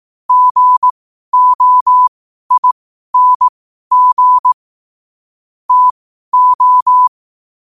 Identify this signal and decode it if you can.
Morse code, heard as a keyed tone, about 18 wpm, spaced as for 12 wpm, 1000 Hz. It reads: GOING TO
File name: qotdfiles / 12wpm